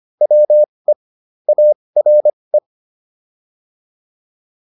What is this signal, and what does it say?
Morse code, heard as a keyed tone, about 25 wpm, spaced as for 20 wpm, 600 Hz. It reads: WE ARE